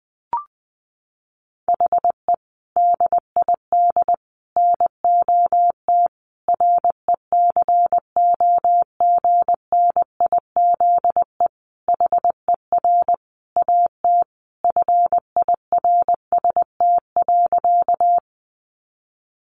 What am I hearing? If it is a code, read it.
Morse code, heard as a keyed tone, about 20 wpm, 700 Hz. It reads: HE DID NOT RECOGNIZE HER AT FIRST.
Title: Sentences from Top 1400 Words 20wpm